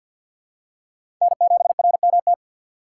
Morse code, heard as a keyed tone, about 47 wpm, 700 Hz. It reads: N7RMT
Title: USA Call Signs - 3x Character Spacing - 50wpm